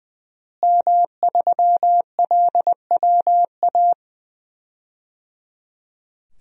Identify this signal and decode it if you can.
Morse code, heard as a keyed tone, about 20 wpm, 700 Hz. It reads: M3LWA
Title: DXCC-100 Call Signs - 5 Characters - Repeated 3x - 20wpm